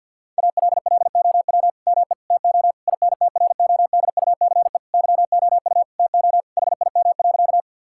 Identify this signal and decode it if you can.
Morse code, heard as a keyed tone, about 50 wpm, 700 Hz. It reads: ALLOW ME TO INTRODUCE YOU TO HIM.
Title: Sentences from Top 1600 Words 50wpm